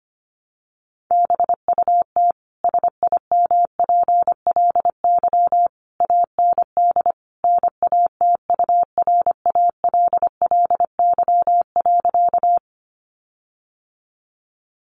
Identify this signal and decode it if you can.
Morse code, heard as a keyed tone, about 25 wpm, 700 Hz. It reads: BUT SIMPLY AND NATURALLY.
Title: Sentences from Top 1300 Words 25wpm